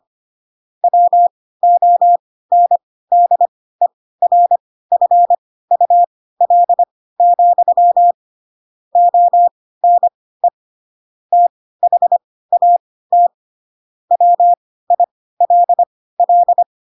Morse code, heard as a keyed tone, about 25 wpm, spaced as for 16 wpm, 700 Hz. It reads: WONDERFUL, ONE THAT WILL
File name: Pinocchio / Pinocchio ch2 10 25 ik1hge